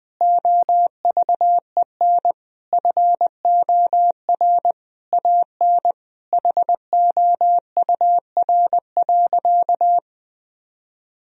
Morse code, heard as a keyed tone, about 21 wpm, 700 Hz. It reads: OVEN FOR AN HOUR.
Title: Sentences from Top 1000 Words - Review Part 4 - 20wpm